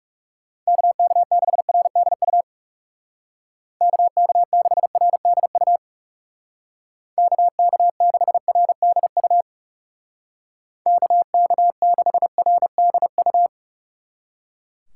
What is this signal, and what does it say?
Morse code, heard as a keyed tone, about 37 wpm, 700 Hz. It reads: KK6RDU KK6RDU KK6RDU KK6RDU
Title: Speed Racing - USA 2x3 Call Signs 30wpm